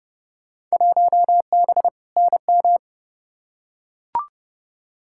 Morse code, heard as a keyed tone, about 30 wpm, 700 Hz. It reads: 1B NM
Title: Field Day Exchanges Encore - 30wpm